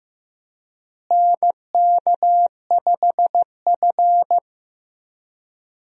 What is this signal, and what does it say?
Morse code, heard as a keyed tone, about 15 wpm, 700 Hz. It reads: NK5F
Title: USA Call Signs - 4 Characters - Repeated 3x - 15wpm